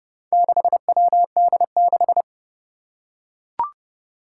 Morse code, heard as a keyed tone, about 28 wpm, 700 Hz. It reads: 6WD6
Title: Four Challenging Letter-Number Combinations 30wpm